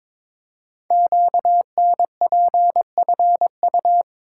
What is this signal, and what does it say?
Morse code, heard as a keyed tone, about 22 wpm, 700 Hz. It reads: QNPFU
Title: Code Groups - 5 Letters - 22wpm